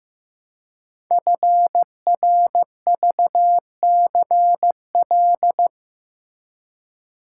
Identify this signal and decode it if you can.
Morse code, heard as a keyed tone, about 15 wpm, 700 Hz. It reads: FRVCL
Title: Code Groups - 5 Letters - 15wpm